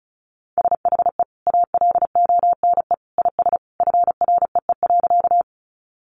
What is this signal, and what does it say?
Morse code, heard as a keyed tone, about 34 wpm, 700 Hz. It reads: SHE ALONE IS FREE.
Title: Sentences from Top 1000 Words 35wpm